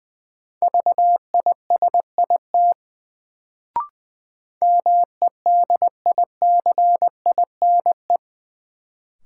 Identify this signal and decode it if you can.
Morse code, heard as a keyed tone, about 20 wpm, 700 Hz. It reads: VISIT MEDICINE
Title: Sentences from Top 900 Words - New Word Drill - 20wpm